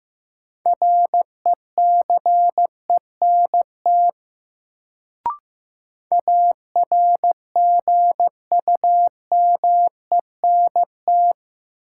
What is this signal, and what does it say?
Morse code, heard as a keyed tone, about 15 wpm, 700 Hz. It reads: RECENT ARGUMENT
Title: Sentences from Top 1400 Words - New Word Drill Alt - 15wpm